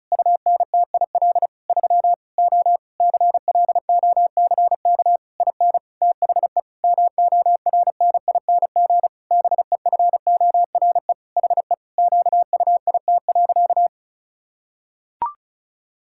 Morse code, heard as a keyed tone, about 36 wpm, 700 Hz. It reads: UNTIL 3 O CLOCK IN THE MORNING BEFORE HE QUIT.
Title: Pangram Challenge v2 - 35wpm